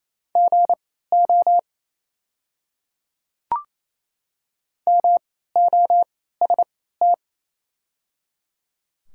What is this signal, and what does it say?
Morse code, heard as a keyed tone, about 28 wpm, spaced as for 16 wpm, 700 Hz. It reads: GO MOST
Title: 3x Character Spacing - Top 100 Words - 28wpm